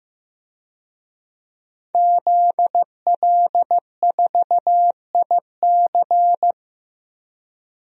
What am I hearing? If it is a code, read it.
Morse code, heard as a keyed tone, about 15 wpm, 700 Hz. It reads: ZL4IC
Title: DXCC-100 Call Signs - 5 Characters - 15wpm